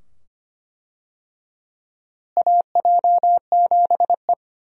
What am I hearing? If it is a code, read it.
Morse code, heard as a keyed tone, about 25 wpm, 700 Hz. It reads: AJ7E
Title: USA Call Signs - 4 Characters - Repeated 2x - 25wpm